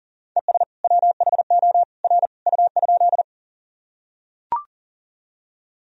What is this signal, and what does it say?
Morse code, heard as a keyed tone, about 37 wpm, 700 Hz. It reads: ES WHO R U?
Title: Sentences from Top 100 Words and Abbreviations 40wpm